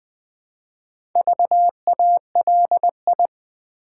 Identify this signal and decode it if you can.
Morse code, heard as a keyed tone, about 20 wpm, 700 Hz. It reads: VALI
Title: Build A Word - 2 Letters At A Time - 8-Letter Words - 20wpm